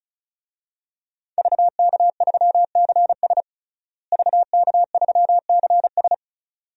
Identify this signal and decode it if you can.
Morse code, heard as a keyed tone, about 34 wpm, 700 Hz. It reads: VK3CS VK3CS
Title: DXCC-100 Call Signs - 5 Characters - Repeated 2x - 35wpm